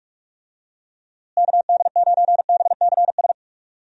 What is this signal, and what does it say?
Morse code, heard as a keyed tone, about 44 wpm, 700 Hz. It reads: KD9BCS
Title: USA Call Signs - 6 Characters - Repeated 3x - 45wpm